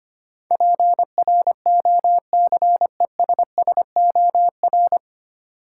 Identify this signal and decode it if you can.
Morse code, heard as a keyed tone, about 25 wpm, 700 Hz. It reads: PROCESSOR
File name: Nine Letter Words - 25wpm